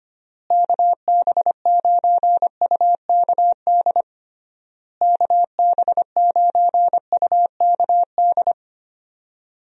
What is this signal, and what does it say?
Morse code, heard as a keyed tone, about 25 wpm, 700 Hz. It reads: KB9UKD KB9UKD
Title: USA Call Signs - 6 Characters - Repeated 3x - 25wpm